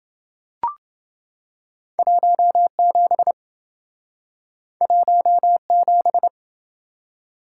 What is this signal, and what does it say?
Morse code, heard as a keyed tone, about 26 wpm, 700 Hz. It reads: 17 17